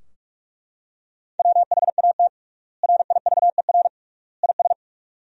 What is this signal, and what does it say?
Morse code, heard as a keyed tone, about 41 wpm, 700 Hz. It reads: WHAT RIVER IS
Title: 3x Word-Spacing - Sentences from Top 300 Words - 45wpm